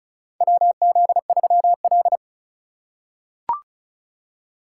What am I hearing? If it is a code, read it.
Morse code, heard as a keyed tone, about 34 wpm, 700 Hz. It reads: WZ3L
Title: USA Call Signs - 3x Character Spacing - 35wpm